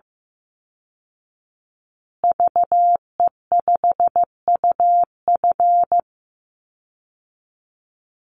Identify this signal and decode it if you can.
Morse code, heard as a keyed tone, about 14 wpm, 700 Hz. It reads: VE5UF
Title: Call Signs Repeated 2x - 15wpm